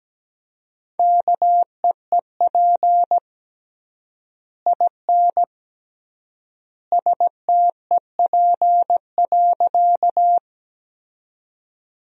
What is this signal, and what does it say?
Morse code, heard as a keyed tone, about 17 wpm, 700 Hz. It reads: KEEP IN STEP.
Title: 3x Word-Spacing - Sentences from Top 400 Words - 17wpm